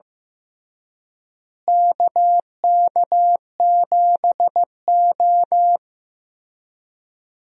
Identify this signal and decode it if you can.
Morse code, heard as a keyed tone, about 15 wpm, 700 Hz. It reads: KK7O
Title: USA Call Signs - 4 Characters - Repeated 3x - 15wpm